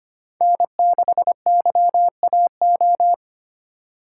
Morse code, heard as a keyed tone, about 25 wpm, 700 Hz. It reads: N6YAO
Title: USA Call Signs - 5 Characters - 3x Character Spacing - 25wpm